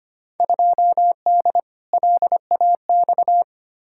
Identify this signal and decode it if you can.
Morse code, heard as a keyed tone, about 25 wpm, 700 Hz. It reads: 2D LAX
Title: Field Day Exchanges - 25wpm